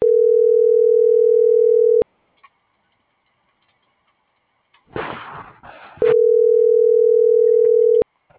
Ringback tone (North America), 440 + 480 Hz.